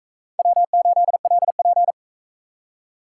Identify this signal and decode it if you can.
Morse code, heard as a keyed tone, about 43 wpm, 700 Hz. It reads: W8LP